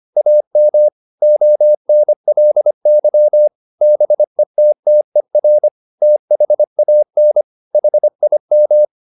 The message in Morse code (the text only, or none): AM ONLY BETTER THAN HIM